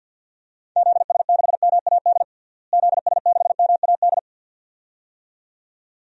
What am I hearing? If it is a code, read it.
Morse code, heard as a keyed tone, about 53 wpm, 700 Hz. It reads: ZS6MAD ZS6MAD
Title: DXCC-100 Call Signs - 6 Characters - Repeated 2x - 50wpm